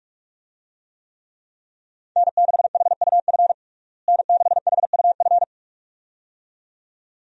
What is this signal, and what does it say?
Morse code, heard as a keyed tone, about 46 wpm, 700 Hz. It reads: N6HUF N6HUF
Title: USA Call Signs - 5 Characters - Repeated 2x - 45wpm